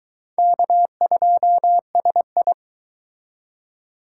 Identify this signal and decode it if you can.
Morse code, heard as a keyed tone, about 24 wpm, 700 Hz. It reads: K2SI